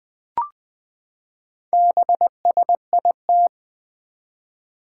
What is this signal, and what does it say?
Morse code, heard as a keyed tone, about 21 wpm, 700 Hz. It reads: BSIT
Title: Four Challenging Letter-Number Combinations 20wpm